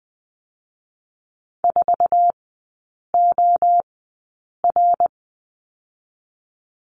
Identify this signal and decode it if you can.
Morse code, heard as a keyed tone, about 20 wpm, spaced as for 8 wpm, 700 Hz. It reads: 4OR
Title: 060 Lesson 11e - Letter h - Three Characters 20wpm